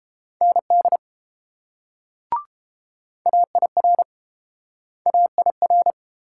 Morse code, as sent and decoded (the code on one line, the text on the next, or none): -. -.. / .- .. .-. / .- .. .-.
ND AIR AIR